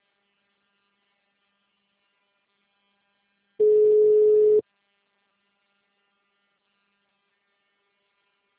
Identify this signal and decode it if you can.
Ringback tone (Europe, 425 Hz).